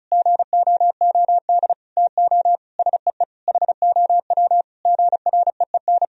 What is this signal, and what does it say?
Morse code, heard as a keyed tone, about 36 wpm, 700 Hz. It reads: GOOD TO SEE HOW GREEN
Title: Sentences from Top 1000 Words - Review Part 2 - 35wpm